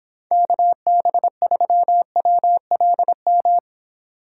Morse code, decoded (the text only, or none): KB3WLM